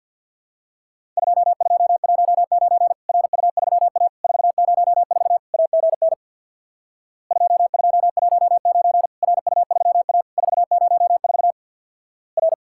Morse code, heard as a keyed tone, about 50 wpm, 700 Hz. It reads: 2219 RU3A 404 AGN 2219 RU3A 404 R
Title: Worked All Europe DX Contest - QTC Exchange with Repeats - 50wpm